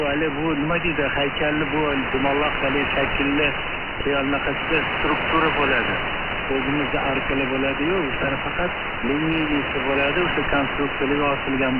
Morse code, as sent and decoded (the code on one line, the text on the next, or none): none